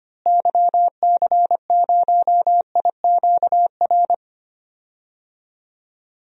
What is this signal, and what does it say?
Morse code, heard as a keyed tone, about 25 wpm, 700 Hz. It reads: YC0IQR
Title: DXCC-100 Call Signs - 6 Characters - Repeated 2x - 25wpm